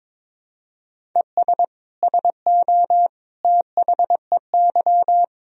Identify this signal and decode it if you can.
Morse code, heard as a keyed tone, about 22 wpm, 700 Hz. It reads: ES SO THEY
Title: Sentences from Top 400 Words and Abbreviations 22wpm